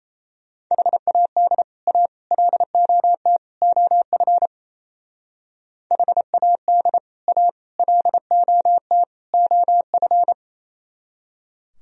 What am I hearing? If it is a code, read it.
Morse code, heard as a keyed tone, about 33 wpm, 700 Hz. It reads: HAD A LOT OF HAD A LOT OF
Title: Speed-Racing - Sets of 4 Words 28wpm